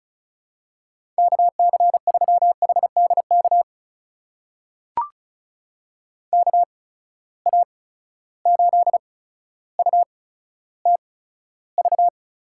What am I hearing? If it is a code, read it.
Morse code, heard as a keyed tone, about 36 wpm, spaced as for 15 wpm, 700 Hz. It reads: KC3HDK KA8UTV
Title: USA Call Signs - 6 Characters - 8x Character Spacing - 35wpm